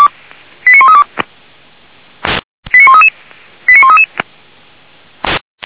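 Five-tone selective call (ZVEI), heard as quick runs of five tones, tones of about 80 ms. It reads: 89132 89133 89133
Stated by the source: ZVEI five-tone selective call received by SDR.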